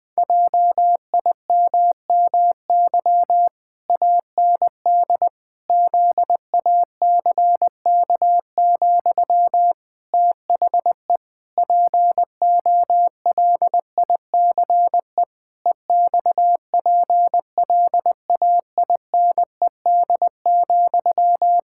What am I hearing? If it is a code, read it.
Morse code, heard as a keyed tone, about 20 wpm, 700 Hz. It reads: JIMMY AND ZACK, THE POLICE EXPLAINED,
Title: Pangram Challenge v2 - 20wpm